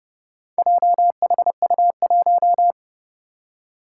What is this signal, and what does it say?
Morse code, heard as a keyed tone, about 29 wpm, 700 Hz. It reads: JHU1
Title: Four Challenging Letter-Number Combinations 30wpm